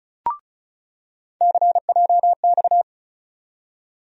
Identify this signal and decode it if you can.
Morse code, heard as a keyed tone, about 34 wpm, 700 Hz. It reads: CJX
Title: USA Call Sign Postfixes - 3 Letters 35wpm